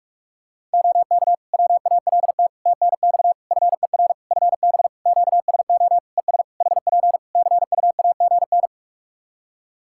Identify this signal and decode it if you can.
Morse code, heard as a keyed tone, about 44 wpm, 700 Hz. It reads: OK WALT TNX FER FB QSO ES HP CUAGN
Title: Practice QSO - Exchange 4 - Ending - 45wpm